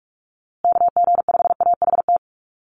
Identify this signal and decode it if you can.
Morse code, heard as a keyed tone, about 44 wpm, 700 Hz. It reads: KG5AHT
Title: USA Call Signs Repeated 3x 45wpm